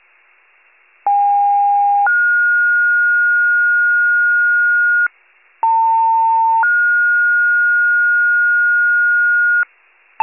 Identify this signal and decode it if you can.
Two-tone page, heard as a one-second tone, then a three-second tone, about 810 then 1430 Hz, about 900 then 1430 Hz. It is heard from 1.1 to 9.6 s.
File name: pager / 73 FIRE / wcalert